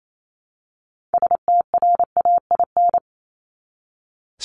Morse code, heard as a keyed tone, about 28 wpm, 700 Hz. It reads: STRAIN
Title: Top 1751-2000 Words 28wpm